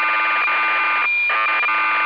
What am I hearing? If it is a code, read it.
Cassette data (Kansas City type).